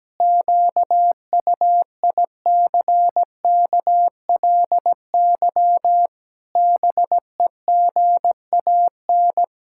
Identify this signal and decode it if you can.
Morse code, heard as a keyed tone, about 17 wpm, 700 Hz. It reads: QUICKLY BEGAN
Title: Pangram Challenge v2 - 17wpm